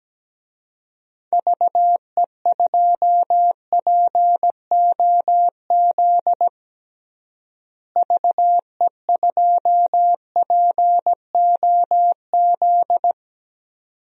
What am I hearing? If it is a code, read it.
Morse code, heard as a keyed tone, about 17 wpm, 700 Hz. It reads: VE2POZ VE2POZ
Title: DXCC-100 Call Signs - 6 Characters - Repeated 2x - 17wpm